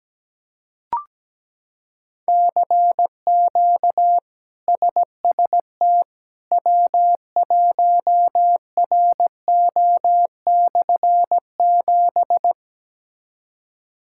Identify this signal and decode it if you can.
Morse code, heard as a keyed tone, about 17 wpm, 700 Hz. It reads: CQ SST W1RO/7
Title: K1USN SST QSO Practice - 17wpm